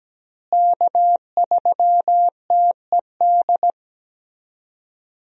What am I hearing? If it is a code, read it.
Morse code, heard as a keyed tone, about 17 wpm, 700 Hz. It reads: K3TED
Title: USA Call Signs - 5 Characters - Repeated 2x - 17wpm